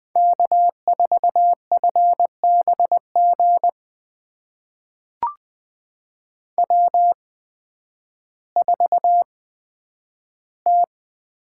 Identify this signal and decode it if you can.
Morse code, heard as a keyed tone, about 20 wpm, spaced as for 11 wpm, 700 Hz. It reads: K4FBG W4T
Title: USA Call Signs - 8x Character Spacing - 20wpm